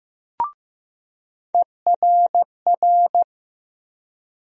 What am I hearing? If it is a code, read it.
Morse code, heard as a keyed tone, about 15 wpm, 700 Hz. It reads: ERR
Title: USA Call Sign Postfixes - 3 Letters 15wpm